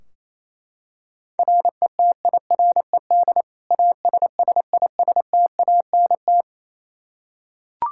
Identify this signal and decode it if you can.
Morse code, heard as a keyed tone, about 28 wpm, 700 Hz. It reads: RETIRED ASSISTANT
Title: K1USN SST QSO Practice 28wpm